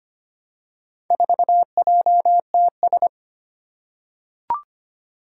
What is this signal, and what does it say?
Morse code, heard as a keyed tone, about 25 wpm, 700 Hz. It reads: 4JTS
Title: Four Challenging Letter-Number Combinations 25wpm